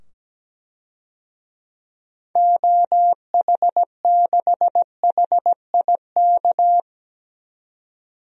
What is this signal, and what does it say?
Morse code, heard as a keyed tone, about 17 wpm, 700 Hz. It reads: OH6HIK
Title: DXCC-100 Call Signs - 6 Characters - 17wpm